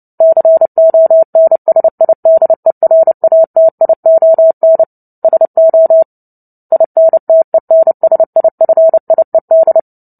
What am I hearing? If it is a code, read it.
Morse code, heard as a keyed tone, about 30 wpm, 650 Hz. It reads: CONSIDERATION SO INTENSIFIED
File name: Book Two / 03-Chapter 3 - 29 WPM